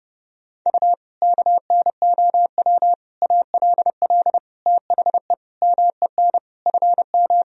Morse code, heard as a keyed tone, about 30 wpm, 700 Hz. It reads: U KNOW ALL THE MEN FM
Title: Sentences from Top 300 Words and Abbreviations 30wpm